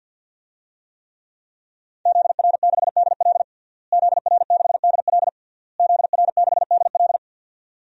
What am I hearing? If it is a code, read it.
Morse code, heard as a keyed tone, about 49 wpm, 700 Hz. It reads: ZR6DL ZR6DL ZR6DL
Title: DXCC-100 Call Signs - 5 Characters - Repeated 3x - 50wpm